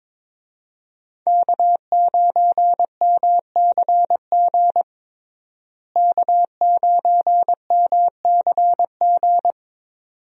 Morse code, heard as a keyed tone, about 22 wpm, 700 Hz. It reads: K9MCG K9MCG
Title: USA Call Signs - 5 Characters - Repeated 2x - 22wpm